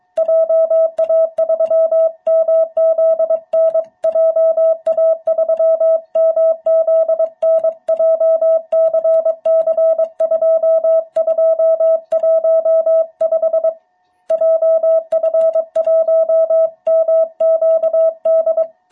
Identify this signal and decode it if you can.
Morse code, heard as a keyed tone, about 23 wpm, 660 Hz. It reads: JA3MZN JA3MZN JCC2215 JF1MQD
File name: jh8cbh / 2021cw